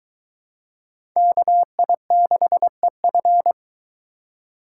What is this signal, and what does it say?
Morse code, heard as a keyed tone, about 22 wpm, 700 Hz. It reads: KI6EF